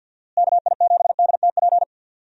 Morse code, heard as a keyed tone, about 50 wpm, 700 Hz. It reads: KI7DTP
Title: USA Call Signs - 6 Characters - Repeated 3x - 50wpm